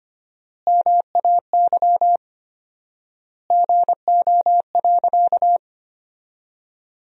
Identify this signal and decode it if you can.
Morse code, heard as a keyed tone, about 25 wpm, 700 Hz. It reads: MAY GO.
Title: 4x Word-Spacing - Sentences from Top 100 Words - 25wpm